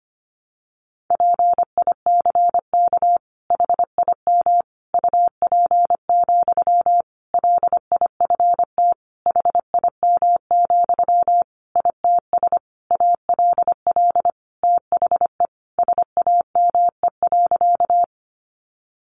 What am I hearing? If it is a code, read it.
Morse code, heard as a keyed tone, about 25 wpm, 700 Hz. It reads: PICK HIM UP, LIFT HIM, ITS ALL THE SAME.
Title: Sentences from Top 1000 Words 25wpm